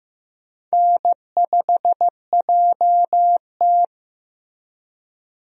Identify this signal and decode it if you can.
Morse code, heard as a keyed tone, about 15 wpm, 700 Hz. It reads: N5JT
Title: USA Call Signs - 4 Characters - Repeated 3x - 15wpm